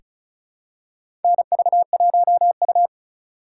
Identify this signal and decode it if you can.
Morse code, heard as a keyed tone, about 36 wpm, 700 Hz. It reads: NV1U
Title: K1USN SST QSO Practice - 35wpm